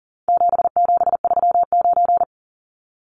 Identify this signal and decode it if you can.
Morse code, heard as a keyed tone, about 40 wpm, 700 Hz. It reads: 7739
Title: Numbers - 4-Digit 40wpm